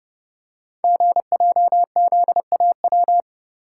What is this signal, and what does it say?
Morse code, heard as a keyed tone, about 28 wpm, 700 Hz. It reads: GJZAW
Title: Code Groups - 5 Letters - 30wpm